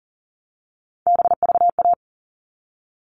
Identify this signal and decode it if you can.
Morse code, heard as a keyed tone, about 40 wpm, 700 Hz. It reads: BVA
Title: All Cut Numbers - 3-Digit 40wpm